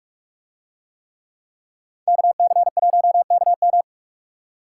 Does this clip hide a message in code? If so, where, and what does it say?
Yes: Morse code from 2.1 to 3.8 s, heard as a keyed tone, about 44 wpm, 700 Hz. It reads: KC1KM